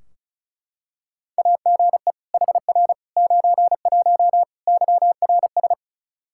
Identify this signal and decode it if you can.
Morse code, heard as a keyed tone, about 34 wpm, 700 Hz. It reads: AGE HR 91 YRS
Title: Practice QSO - Exchange 3 - More Info - 35wpm